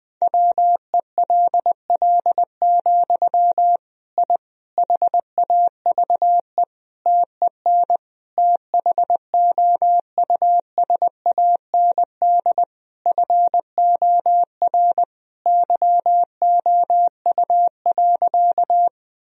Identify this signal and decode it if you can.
Morse code, heard as a keyed tone, about 20 wpm, 700 Hz. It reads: WELL, I HAVE TEN THOUSAND FOR YOU.
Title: Sentences from Top 1000 Words - Review Part 3 - 20wpm